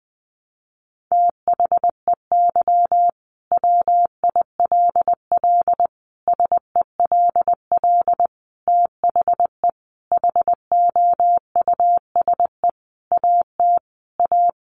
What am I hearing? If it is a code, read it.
Morse code, heard as a keyed tone, about 20 wpm, 700 Hz. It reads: THEY WILL SELL THE HOUSE AT A
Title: Sentences from Top 1200 Words 20wpm